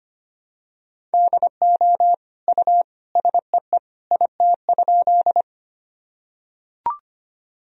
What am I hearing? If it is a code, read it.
Morse code, heard as a keyed tone, about 25 wpm, 700 Hz. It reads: DO U SEE IT?
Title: Sentences from Top 100 Words and Abbreviations 25wpm